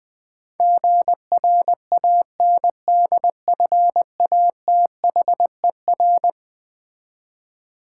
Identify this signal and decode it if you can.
Morse code, heard as a keyed tone, about 20 wpm, 700 Hz. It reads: GRANDFATHER
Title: Eleven Letter Words - 20wpm